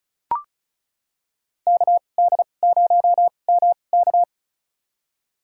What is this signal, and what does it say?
Morse code, heard as a keyed tone, about 36 wpm, spaced as for 25 wpm, 700 Hz. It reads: KD0MK
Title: USA Call Signs - 5 Characters - 2x Character Spacing - 35wpm